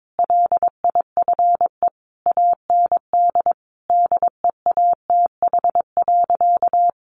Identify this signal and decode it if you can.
Morse code, heard as a keyed tone, about 22 wpm, 700 Hz. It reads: LIFE AND DEATH.
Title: Sentences from Top 1100 Words 22wpm